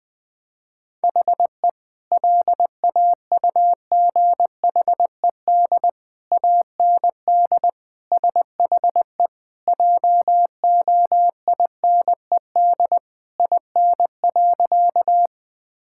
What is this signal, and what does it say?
Morse code, heard as a keyed tone, about 21 wpm, 700 Hz. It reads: HE LAUGHED AND SHE JOINED IN.
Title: Sentences with 2-Word Phrases - 20wpm